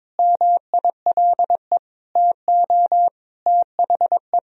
Morse code, heard as a keyed tone, about 21 wpm, 700 Hz. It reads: MILE TO THE
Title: Sentences from Top 1000 Words - Review Part 1 - 22wpm